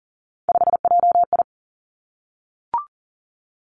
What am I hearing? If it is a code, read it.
Morse code, heard as a keyed tone, about 40 wpm, 700 Hz. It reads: 5JI
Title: Three Challenging Letter-Number Combinations 40wpm